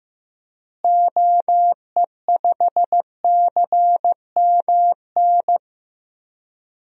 Morse code, heard as a keyed tone, about 15 wpm, 700 Hz. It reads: OE5CMN
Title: DXCC-100 Call Signs - 6 Characters - Repeated 2x - 15wpm